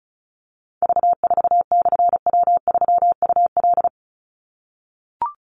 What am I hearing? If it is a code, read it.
Morse code, heard as a keyed tone, about 34 wpm, 700 Hz. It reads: V4/W3UL
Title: Call Signs - 35wpm